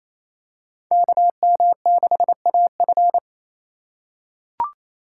Morse code, heard as a keyed tone, about 27 wpm, 700 Hz. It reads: KM6AF
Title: USA Call Signs - 2x Character Spacing - 28wpm